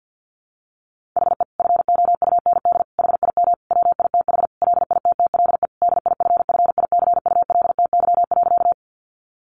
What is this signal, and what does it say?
Morse code, heard as a keyed tone, about 50 wpm, 700 Hz. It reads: HE FOUND HIM WITH LITTLE DIFFICULTY.
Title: Sentences from Top 1500 Words 50wpm